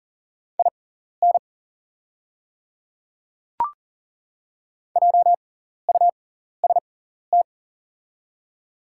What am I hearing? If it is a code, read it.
Morse code, heard as a keyed tone, about 40 wpm, spaced as for 14 wpm, 690 Hz. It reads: IN JUST
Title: 6x Character Spacing - Top 200 Words - 40wpm